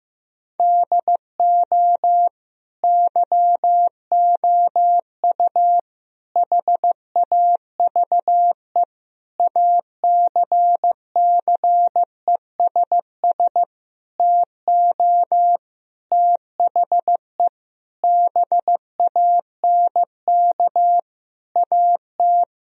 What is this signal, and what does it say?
Morse code, heard as a keyed tone, about 15 wpm, 700 Hz. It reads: DO YOU HAVE ACCESS TO THE BANK AT
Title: Sentences from Top 1000 Words - Review Part 4 - 15wpm